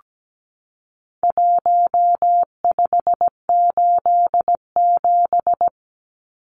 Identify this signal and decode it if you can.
Morse code, heard as a keyed tone, about 17 wpm, 700 Hz. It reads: 1587
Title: Numbers - 4-Digit 17wpm